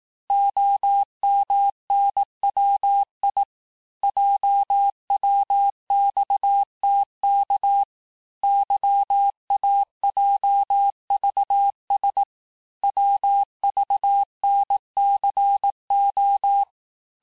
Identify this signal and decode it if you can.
Morse code, heard as a keyed tone, about 18 wpm, 800 Hz. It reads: OMNWI JWXTK YAJVS WVNCO
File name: cours-du-lundi files / cours-1588-texte-chiffre-vitesse-17MPM